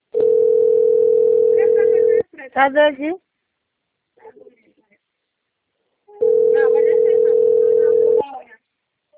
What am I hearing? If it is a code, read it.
Ringback tone (North America), 440 + 480 Hz, with speech.
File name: rg-1600-unknown- - - .46684